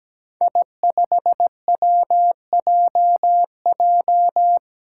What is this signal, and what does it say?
Morse code, heard as a keyed tone, about 17 wpm, 700 Hz. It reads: I5WJJ
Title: Five Challenging Letter-Number Combinations 17wpm